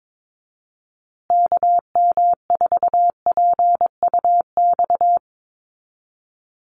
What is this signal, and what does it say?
Morse code, heard as a keyed tone, about 22 wpm, 700 Hz. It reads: KM4PUX
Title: USA Call Signs Repeated 3x 22wpm